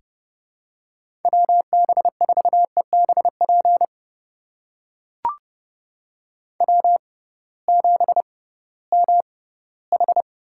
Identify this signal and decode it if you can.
Morse code, heard as a keyed tone, about 30 wpm, spaced as for 18 wpm, 700 Hz. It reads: WB4EBP W7MH
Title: USA Call Signs - 6x Character Spacing - 30wpm